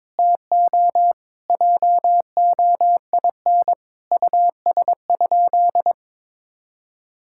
Morse code, heard as a keyed tone, about 22 wpm, 700 Hz. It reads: TO JOIN US?
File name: practice / 22wpm / Sentences with 4-Word Phrases - 22wpm